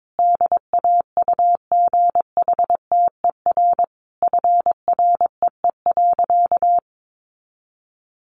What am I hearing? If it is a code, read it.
Morse code, heard as a keyed tone, about 22 wpm, 700 Hz. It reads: DAUGHTER FREE.
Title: Sentences from Top 1500 Words 22wpm